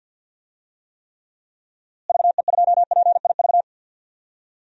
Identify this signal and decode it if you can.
Morse code, heard as a keyed tone, about 53 wpm, 700 Hz. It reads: VE2PIV
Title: DXCC-100 Call Signs - 6 Characters - Repeated 3x - 50wpm